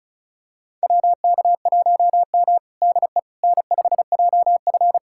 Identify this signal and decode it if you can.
Morse code, heard as a keyed tone, about 36 wpm, 700 Hz. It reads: WK1M DE N5JF
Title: USA Callsign DE Callsign - 4 Characters - 35wpm